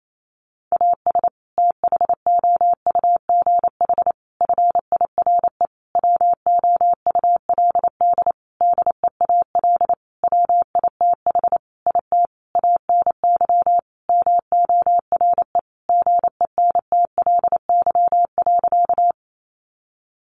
Morse code, heard as a keyed tone, about 28 wpm, 700 Hz. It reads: AS THOUGH FIRE WOULD DEAL WITH IT ANY MORE GENTLY.
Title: Sentences from Top 1500 Words 28wpm